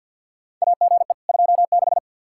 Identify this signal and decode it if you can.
Morse code, heard as a keyed tone, about 47 wpm, 700 Hz. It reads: AGE 26
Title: Practice QSO - Exchange 3 - More Info - 50wpm